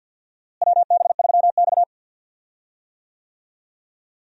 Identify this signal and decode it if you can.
Morse code, heard as a keyed tone, about 52 wpm, 700 Hz. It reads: WB3X
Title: USA Call Signs - 4 Characters - Repeated 3x - 50wpm